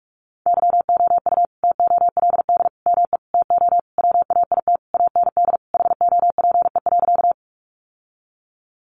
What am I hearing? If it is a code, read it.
Morse code, heard as a keyed tone, about 44 wpm, 700 Hz. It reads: YOU TOLD ME TO WAIT AND HOPE.
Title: Sentences from Top 1000 Words 45wpm